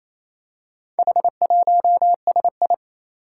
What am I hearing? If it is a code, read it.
Morse code, heard as a keyed tone, about 28 wpm, 700 Hz. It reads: H1SI
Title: Four Challenging Letter-Number Combinations 28wpm